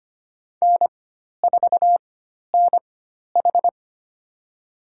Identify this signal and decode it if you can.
Morse code, heard as a keyed tone, about 25 wpm, spaced as for 12 wpm, 700 Hz. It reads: N4NH
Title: USA Call Signs - 4x Character Spacing - 25wpm